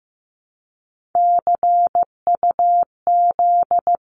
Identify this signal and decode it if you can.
Morse code, heard as a keyed tone, about 15 wpm, 700 Hz. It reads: CUZ
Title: Abbreviations 15wpm